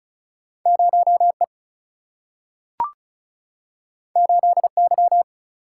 Morse code, heard as a keyed tone, about 34 wpm, 700 Hz. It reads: 0E 8Y
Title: Single-Letter Single-Number 35wpm